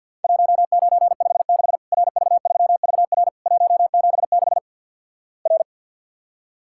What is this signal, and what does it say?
Morse code, heard as a keyed tone, about 48 wpm, 700 Hz. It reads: 1956 RV3VR 176 R
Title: Worked All Europe DX Contest - QTC Exchange with Repeats - 50wpm